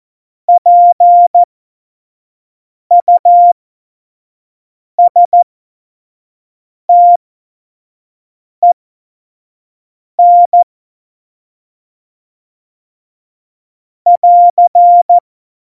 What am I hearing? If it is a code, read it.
Morse code, heard as a keyed tone, about 14 wpm, spaced as for 5 wpm, 700 Hz. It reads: PUSTEN <AR>
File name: Morsekurs / Witze / Witz 14. August 2018, 25 F70